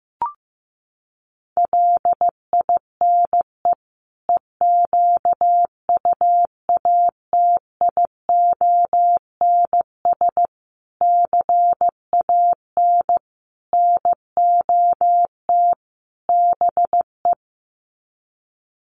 Morse code, heard as a keyed tone, about 15 wpm, 700 Hz. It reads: LINE EQUATIONS CAN NOT BE
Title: Sets of 5 Words 15wpm